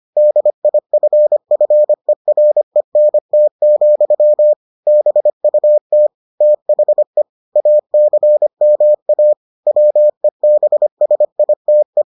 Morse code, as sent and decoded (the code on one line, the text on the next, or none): -.. .. ..-. ..-. .-. -. - --..-- / -... ..- - / - .... / .- -.-. -- .- / .-- . -... ... .. - .
DIFFERENT, BUT THE ACMA WEBSITE